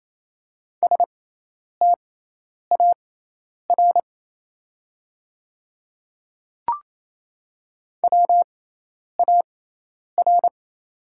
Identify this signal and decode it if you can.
Morse code, heard as a keyed tone, about 28 wpm, spaced as for 10 wpm, 690 Hz. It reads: STAR WAR
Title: 6x Character Spacing - Top 500 Words - 28wpm